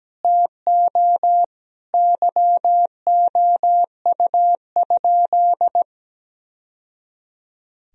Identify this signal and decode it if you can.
Morse code, heard as a keyed tone, about 17 wpm, 700 Hz. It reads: TO YOU?
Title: Sentences from Top 1000 Words - Review Part 2 - 17wpm